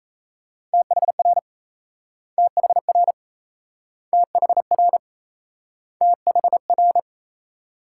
Morse code, heard as a keyed tone, about 38 wpm, 700 Hz. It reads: THR THR THR THR